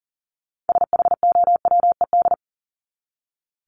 Morse code, heard as a keyed tone, about 40 wpm, 700 Hz. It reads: SHOWED
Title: ED Suffix - Top 100 Words with Base Words 40wpm